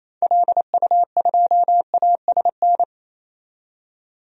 Morse code, heard as a keyed tone, about 28 wpm, 700 Hz. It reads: LU2ASN
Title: DXCC-100 Call Signs - 6 Characters - 28wpm